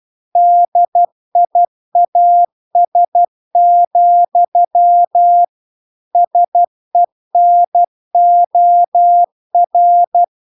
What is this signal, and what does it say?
Morse code, heard as a keyed tone, about 12 wpm, 700 Hz. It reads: DIAS, SENOR